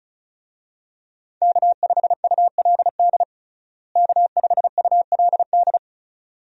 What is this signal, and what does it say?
Morse code, heard as a keyed tone, about 34 wpm, 700 Hz. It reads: K5ULD K5ULD
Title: USA Call Signs - 5 Characters - Repeated 2x - 35wpm